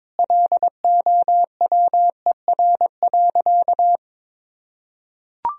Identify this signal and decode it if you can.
Morse code, heard as a keyed tone, about 22 wpm, 700 Hz. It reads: LOWER.
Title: Sentences from Top 1000 Words - Review Part 4 - 22wpm